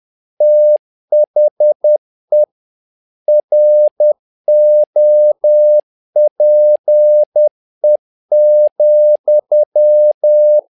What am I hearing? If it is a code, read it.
Morse code, heard as a keyed tone, about 10 wpm, 600 Hz. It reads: THE ROPE,
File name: learning center / tower trouble / tower trouble 10wpm